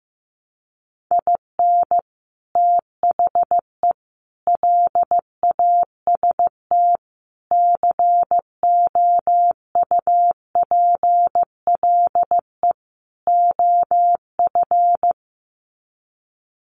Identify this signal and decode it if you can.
Morse code, heard as a keyed tone, about 14 wpm, 700 Hz. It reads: IN THE LAST COUPLE OF
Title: Sets of 5 Words 15wpm